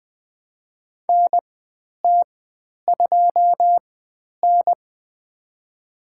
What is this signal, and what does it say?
Morse code, heard as a keyed tone, about 20 wpm, spaced as for 10 wpm, 700 Hz. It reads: NT2N
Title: 095 Lesson 16g - Letter c - Call Signs 20wpm